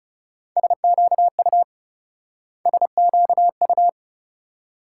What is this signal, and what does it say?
Morse code, heard as a keyed tone, about 36 wpm, 700 Hz. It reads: SQU SQU
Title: Speed-Racing Multi-letter Phonemes - 30wpm